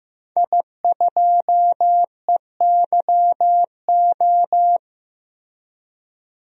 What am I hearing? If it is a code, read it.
Morse code, heard as a keyed tone, about 15 wpm, 700 Hz. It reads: I2EYO
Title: DXCC-100 Call Signs - 5 Characters - 15wpm